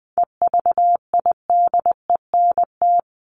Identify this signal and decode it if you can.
Morse code, heard as a keyed tone, about 20 wpm, 700 Hz. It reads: EVIDENT
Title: LY Suffix - Top 100 Words with Base Words 20wpm